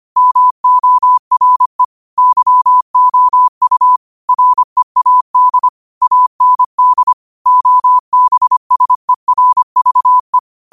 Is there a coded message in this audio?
Yes: Morse code from 0.2 to 10.4 s, heard as a keyed tone, about 25 wpm, 1000 Hz. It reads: MORE YOU READ AND OBSERVE